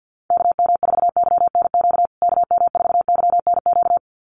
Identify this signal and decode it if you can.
Morse code, heard as a keyed tone, about 49 wpm, 700 Hz. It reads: KM4YNQ KM4YNQ
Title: USA Call Signs Repeated 2x 50wpm